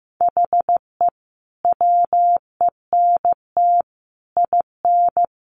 Morse code, heard as a keyed tone, about 15 wpm, 700 Hz. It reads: HE WENT IN
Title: Sentences from Top 1400 Words 15wpm